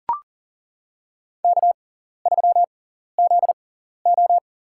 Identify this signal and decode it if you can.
Morse code, heard as a keyed tone, about 40 wpm, spaced as for 14 wpm, 700 Hz. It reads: K3ZO
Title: USA Call Signs - 4 Characters - 6x Character Spacing - 40wpm